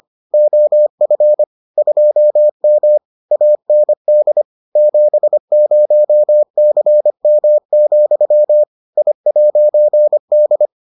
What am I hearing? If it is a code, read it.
Morse code, heard as a keyed tone, about 25 wpm, 600 Hz. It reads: OF 2M AND 70CM, I'D